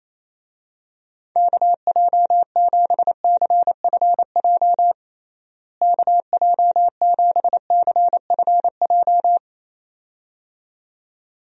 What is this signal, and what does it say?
Morse code, heard as a keyed tone, about 28 wpm, 700 Hz. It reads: KJ7CFJ KJ7CFJ
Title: USA Call Signs - 6 Characters - Repeated 2x - 28wpm